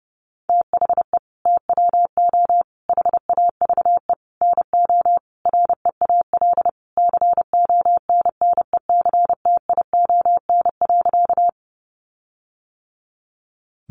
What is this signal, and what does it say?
Morse code, heard as a keyed tone, about 30 wpm, 700 Hz. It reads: THE TWO HAVE NO REAL CONNECTION.
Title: Sentences from Top 1300 Words 30wpm